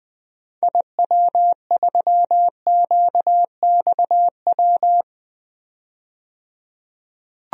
Morse code, heard as a keyed tone, about 20 wpm, 700 Hz. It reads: IW3QXW
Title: DXCC-100 Call Signs - 6 Characters - 20wpm